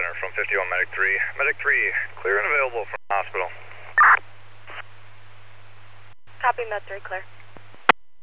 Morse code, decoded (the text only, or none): none